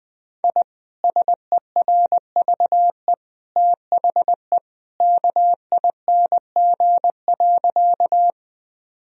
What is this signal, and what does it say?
Morse code, heard as a keyed tone, about 21 wpm, 700 Hz. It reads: I SERVE THE KING.
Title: Sentences from Top 1000 Words - Review Part 2 - 20wpm